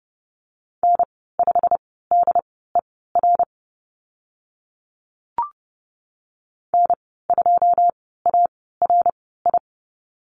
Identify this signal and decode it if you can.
Morse code, heard as a keyed tone, about 30 wpm, spaced as for 17 wpm, 700 Hz. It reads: N5DER N2ARI
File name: Lesson 15g - Number 5 - Call Signs 30wpm